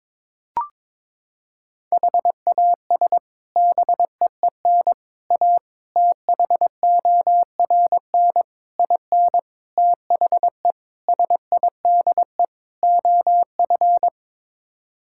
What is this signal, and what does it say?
Morse code, heard as a keyed tone, about 22 wpm, 700 Hz. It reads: HAS BEEN A THORN IN THE SIDE OF
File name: practice / 22wpm / Sets of 8 Words - 22wpm